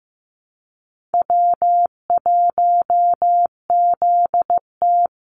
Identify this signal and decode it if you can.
Morse code, heard as a keyed tone, about 15 wpm, 700 Hz. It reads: W1ZT
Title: Easy Call Signs Repeated 2x - 15wpm